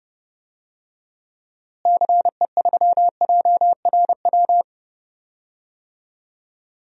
Morse code, heard as a keyed tone, about 30 wpm, 700 Hz. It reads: CE3JRW
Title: DXCC-100 Call Signs - 6 Characters - Repeated 2x - 30wpm